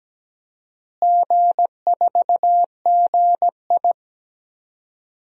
Morse code, heard as a keyed tone, about 17 wpm, 700 Hz. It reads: G4GI
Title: DXCC-100 Call Signs - 4 Characters - Repeated 3x - 17wpm